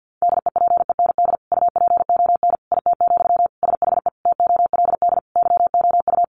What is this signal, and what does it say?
Morse code, heard as a keyed tone, about 50 wpm, 700 Hz. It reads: DEPEND UPON IT, SHE TOLD YOU
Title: Sentences from Top 1300 Words 50wpm